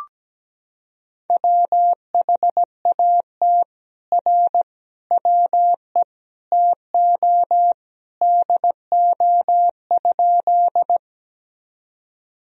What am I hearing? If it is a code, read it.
Morse code, heard as a keyed tone, about 17 wpm, 700 Hz. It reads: WHAT R WE TO DO?
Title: Sentences from Top 100 Words and Abbreviations 17wpm